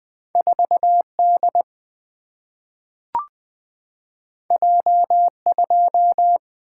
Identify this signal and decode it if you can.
Morse code, heard as a keyed tone, about 20 wpm, 700 Hz. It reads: 4 D J 2
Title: Single-Letter Single-Number 20wpm